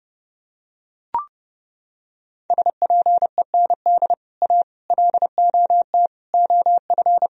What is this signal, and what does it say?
Morse code, heard as a keyed tone, about 29 wpm, 700 Hz. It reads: SPEND A LOT OF